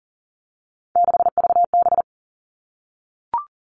Morse code, heard as a keyed tone, about 40 wpm, 700 Hz. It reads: 6VB
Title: All Cut Numbers - 3-Digit 40wpm